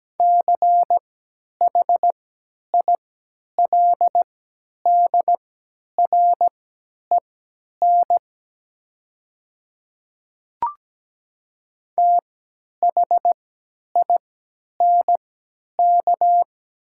Morse code, heard as a keyed tone, about 17 wpm, spaced as for 10 wpm, 690 Hz. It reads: CHILDREN THINK
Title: 3x Character Spacing - Top 500 Words - 17wpm